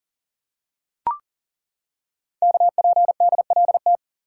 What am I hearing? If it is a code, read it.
Morse code, heard as a keyed tone, about 42 wpm, 690 Hz. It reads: KPDLT
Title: Code Groups - 5 Letters - 40wpm